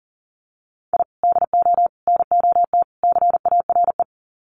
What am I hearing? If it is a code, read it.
Morse code, heard as a keyed tone, about 40 wpm, 700 Hz. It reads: I DO NOT CARE
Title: Sets of 4 Words 40wpm